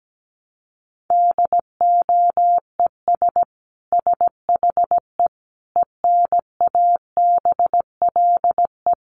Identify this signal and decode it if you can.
Morse code, heard as a keyed tone, about 17 wpm, 700 Hz. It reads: DOES SHE ENABLE
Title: Sentences from Top 1500 Words 17wpm